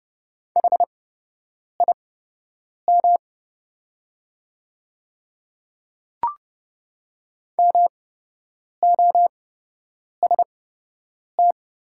Morse code, heard as a keyed tone, about 30 wpm, spaced as for 8 wpm, 700 Hz. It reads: HIM MOST
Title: 8x Character Spacing - Top 100 Words - 30wpm